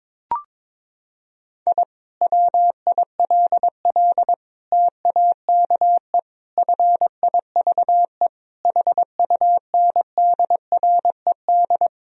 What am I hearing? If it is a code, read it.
Morse code, heard as a keyed tone, about 22 wpm, 700 Hz. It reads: I WILL TAKE FIVE HUNDRED
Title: Sentences from Top 1000 Words - Review Part 2 - 22wpm